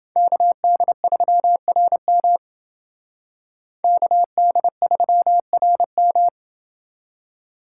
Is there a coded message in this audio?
Yes: Morse code from 0.2 to 6.3 s, heard as a keyed tone, about 29 wpm, 700 Hz. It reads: KD3RM KD3RM